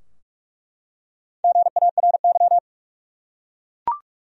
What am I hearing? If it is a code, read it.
Morse code, heard as a keyed tone, about 46 wpm, 700 Hz. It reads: GARY
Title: USA Common Names - Max 5 Letters - 45wpm